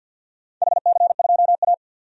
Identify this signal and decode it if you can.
Morse code, heard as a keyed tone, about 47 wpm, 700 Hz. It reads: HC2A
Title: DXCC-100 Call Signs - 4 Characters - 50wpm